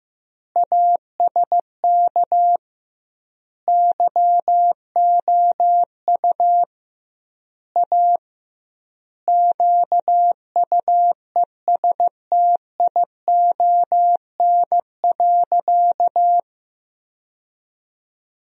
Morse code, heard as a keyed tone, about 16 wpm, 700 Hz. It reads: ASK YOU A QUESTION.
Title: 2x Word-Spacing - Sentences from Top 400 Words - 15wpm